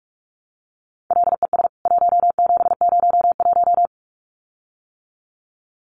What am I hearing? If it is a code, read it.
Morse code, heard as a keyed tone, about 44 wpm, 700 Hz. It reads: LES 1701
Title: CWT Exchange 45wpm